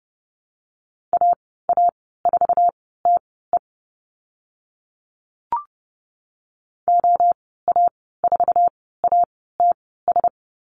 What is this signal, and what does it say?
Morse code, heard as a keyed tone, about 30 wpm, spaced as for 17 wpm, 700 Hz. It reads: AA4TE OA4ATS
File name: Lesson 15g - Number 5 - Call Signs 30wpm